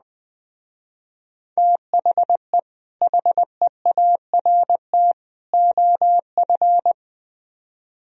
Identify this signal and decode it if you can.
Morse code, heard as a keyed tone, about 20 wpm, 700 Hz. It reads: THE HEART OF
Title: Speed-Racing - Sets of 3 Words 17wpm